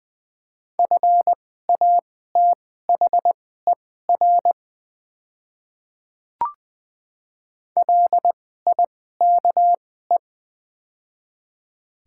Morse code, heard as a keyed tone, about 20 wpm, spaced as for 14 wpm, 700 Hz. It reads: FATHER LIKE
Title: 2x Character Spacing - Top 300 Words - 20wpm